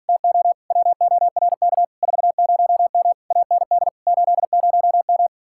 Morse code, heard as a keyed tone, about 46 wpm, 700 Hz. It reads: TO WORK 40M AND 80M